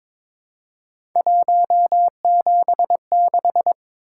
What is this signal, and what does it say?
Morse code, heard as a keyed tone, about 23 wpm, 700 Hz. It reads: 176
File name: practice / 22wpm / Three Letter-Number Combinations - 22wpm